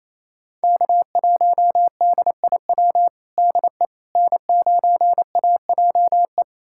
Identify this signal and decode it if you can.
Morse code, heard as a keyed tone, about 28 wpm, 700 Hz. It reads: K1DIW DE N9AJE
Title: USA Callsign DE Callsign - 5 Characters - 28wpm